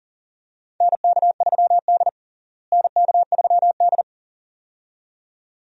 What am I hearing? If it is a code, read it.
Morse code, heard as a keyed tone, about 40 wpm, 700 Hz. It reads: NK3D NK3D
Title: USA Call Signs - 4 Characters - Repeated 2x - 40wpm